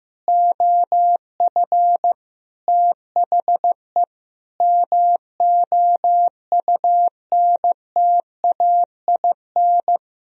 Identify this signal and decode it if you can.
Morse code, heard as a keyed tone, about 15 wpm, 700 Hz. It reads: OF THE MOUNTAIN
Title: Sentences from Top 1000 Words - Review Part 3 - 15wpm